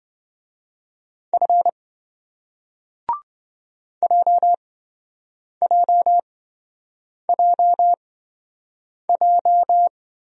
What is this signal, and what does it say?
Morse code, heard as a keyed tone, about 28 wpm, spaced as for 7 wpm, 700 Hz. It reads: F JJJJ